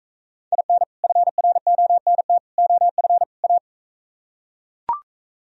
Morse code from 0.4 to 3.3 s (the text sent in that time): IN FRONT OF